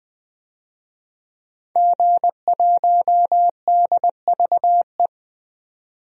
Morse code, heard as a keyed tone, about 20 wpm, 700 Hz. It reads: G1DVE
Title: DXCC-100 Call Signs - 5 Characters - 20wpm